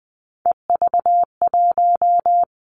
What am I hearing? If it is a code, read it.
Morse code, heard as a keyed tone, about 20 wpm, 700 Hz. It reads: EV1
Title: Three Challenging Letter-Number Combinations 20wpm